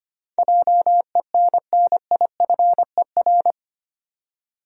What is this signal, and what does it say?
Morse code, heard as a keyed tone, about 25 wpm, 700 Hz. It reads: JENNIFER
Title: USA Common Names - 25wpm